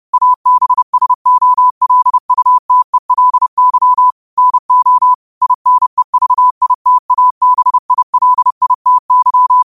Morse code, heard as a keyed tone, about 30 wpm, 1000 Hz. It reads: ABSOLUTELY NO INEVITABILITY
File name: qotdfiles / 30wpm